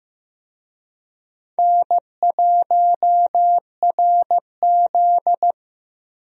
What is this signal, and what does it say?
Morse code, heard as a keyed tone, about 15 wpm, 700 Hz. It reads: N1RZ
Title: USA Call Signs - 4 Characters - Repeated 3x - 15wpm